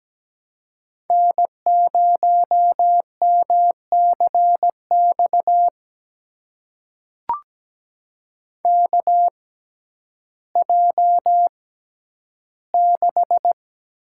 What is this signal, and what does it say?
Morse code, heard as a keyed tone, about 17 wpm, spaced as for 10 wpm, 700 Hz. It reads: N0MCX KJ6
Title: USA Call Signs - 5 Characters - 6x Character Spacing - 17wpm